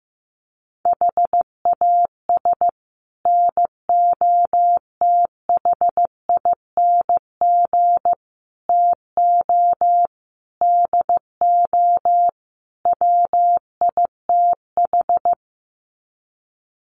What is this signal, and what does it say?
Morse code, heard as a keyed tone, about 15 wpm, 700 Hz. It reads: HAS NOTHING TO DO WITH
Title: Sets of 5 Words 15wpm